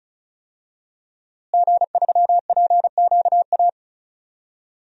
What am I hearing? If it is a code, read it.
Morse code, heard as a keyed tone, about 36 wpm, 700 Hz. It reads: G3PQA
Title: DXCC-100 Call Signs - 5 Characters - Repeated 2x - 35wpm